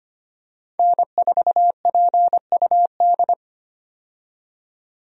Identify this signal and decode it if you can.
Morse code, heard as a keyed tone, about 25 wpm, 700 Hz. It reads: N4PUD
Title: USA Call Signs - 5 Characters - 25wpm